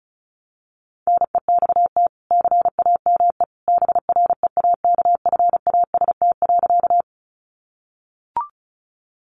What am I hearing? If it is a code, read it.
Morse code, heard as a keyed tone, about 34 wpm, 700 Hz. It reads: NEXT CAME BREAKFAST.
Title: Sentences from Top 1500 Words 35wpm